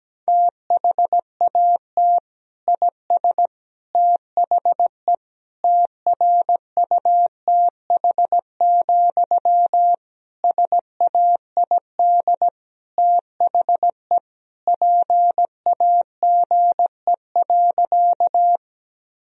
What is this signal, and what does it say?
Morse code, heard as a keyed tone, about 17 wpm, 700 Hz. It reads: THAT IS THE TRUTH, SAID THE PAGE.
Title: Sentences from Top 1000 Words - Review Part 1 - 17wpm